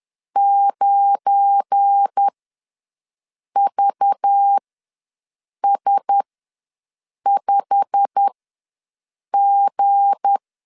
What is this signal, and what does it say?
Morse code, heard as a keyed tone, about 10 wpm, spaced as for 6 wpm, 800 Hz. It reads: 9VS5G
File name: LEZIONI / 8 YJ45 / 8 YJ45 ws 10wpm 3843